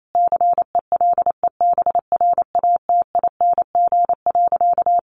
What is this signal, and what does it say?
Morse code, heard as a keyed tone, about 28 wpm, 700 Hz. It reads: CELEBRATING.
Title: A Sound Of Thunder 28wpm